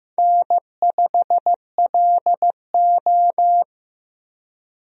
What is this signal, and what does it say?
Morse code, heard as a keyed tone, about 15 wpm, 700 Hz. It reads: N5LO
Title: USA Call Signs - 4 Characters - Repeated 2x - 15wpm